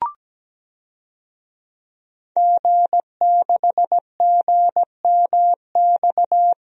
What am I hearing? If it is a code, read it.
Morse code, heard as a keyed tone, about 17 wpm, 700 Hz. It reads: G6GMX
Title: DXCC-100 Call Signs - 5 Characters - Repeated 3x - 17wpm